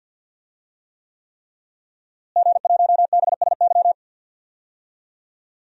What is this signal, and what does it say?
Morse code, heard as a keyed tone, about 47 wpm, 700 Hz. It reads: G1BSY
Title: DXCC-100 Call Signs - 5 Characters - Repeated 3x - 50wpm